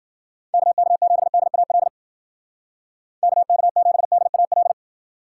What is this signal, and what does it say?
Morse code, heard as a keyed tone, about 49 wpm, 700 Hz. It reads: KK7DAL KK7DAL
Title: Speed Racing - USA 2x3 Call Signs 40wpm